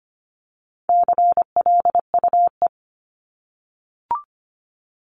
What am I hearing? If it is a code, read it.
Morse code, heard as a keyed tone, about 25 wpm, 700 Hz. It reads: CLUE
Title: Top 1751-2000 Words 25wpm